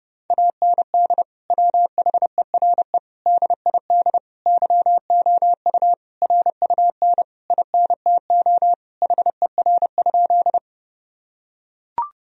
Morse code, heard as a keyed tone, about 30 wpm, 700 Hz. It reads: AND WHERE DID YOU RUN INTO HER?
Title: Sentences from Top 1000 Words - Review Part 1 - 30wpm